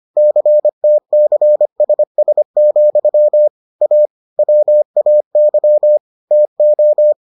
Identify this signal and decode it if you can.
Morse code, heard as a keyed tone, about 25 wpm, 600 Hz. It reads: CTCSS, A WAY TO